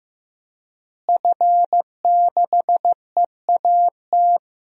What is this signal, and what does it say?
Morse code, heard as a keyed tone, about 15 wpm, 700 Hz. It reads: F6EAT
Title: DXCC-100 Call Signs - 5 Characters - Repeated 3x - 15wpm